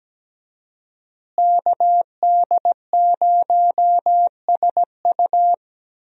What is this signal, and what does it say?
Morse code, heard as a keyed tone, about 17 wpm, 700 Hz. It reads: KD0SU
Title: USA Call Signs - 5 Characters - Repeated 2x - 17wpm